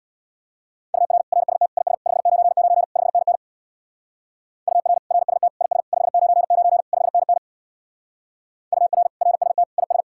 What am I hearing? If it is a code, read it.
Morse code, heard as a keyed tone, about 59 wpm, 700 Hz. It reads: UR RST IS 599 5NN UR RST IS 599 5NN UR RST IS
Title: Speed-Racing Common QSO Phrases Encore - 50wpm